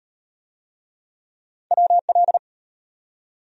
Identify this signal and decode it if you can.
Morse code, heard as a keyed tone, about 36 wpm, 700 Hz. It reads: WL